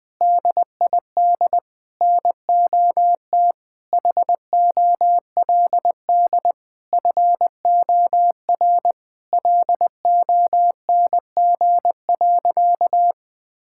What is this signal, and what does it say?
Morse code, heard as a keyed tone, about 20 wpm, 700 Hz. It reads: DID NOT HOLD FOR LONG.
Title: Sentences from Top 1000 Words - Review Part 5 - 20wpm